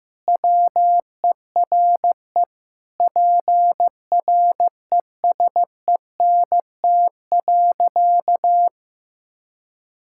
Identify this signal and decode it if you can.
Morse code, heard as a keyed tone, about 15 wpm, 700 Hz. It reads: WERE PRESENT.
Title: Sentences from Top 1000 Words - Review Part 5 - 15wpm